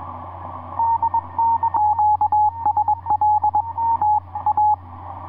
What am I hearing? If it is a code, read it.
Morse code, heard as a keyed tone, about 21 wpm, 840 Hz, with radio noise. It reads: QSL T A
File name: dx1j dl6kai etc 40m cw 13jan2010 2235